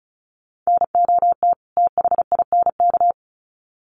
Morse code, heard as a keyed tone, about 34 wpm, 700 Hz. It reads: NOT THINK
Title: Sets of 2 Words 35wpm